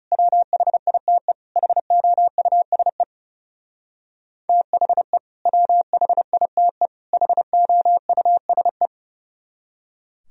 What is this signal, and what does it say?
Morse code, heard as a keyed tone, about 36 wpm, 700 Hz. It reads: WHITE HOUSE THE WHITE HOUSE
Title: Speed-Racing - Sets of 3 Words 30wpm